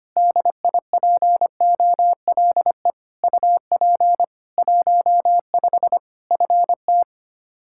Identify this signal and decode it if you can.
Morse code, heard as a keyed tone, about 25 wpm, 700 Hz. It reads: DIPOLE UP 15 FT
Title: Practice QSO - Exchange 2 - Info - 25wpm